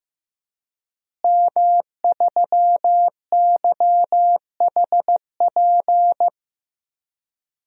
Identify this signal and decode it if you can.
Morse code, heard as a keyed tone, about 16 wpm, 700 Hz. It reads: M3YHP
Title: DXCC-100 Call Signs - 5 Characters - Repeated 2x - 15wpm